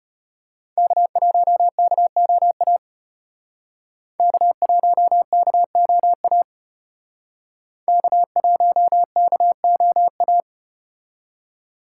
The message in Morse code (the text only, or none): K1KOA K1KOA K1KOA